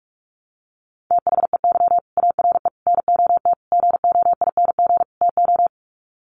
Morse code, heard as a keyed tone, about 44 wpm, 700 Hz. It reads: THEY ARE NOT GOING TO
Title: Sets of 5 Words Encore2 45wpm